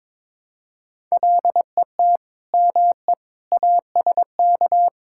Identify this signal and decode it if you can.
Morse code, heard as a keyed tone, about 21 wpm, 700 Hz. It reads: LET ME ASK
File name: practice / 22wpm / Sentences with 4-Word Phrases - 22wpm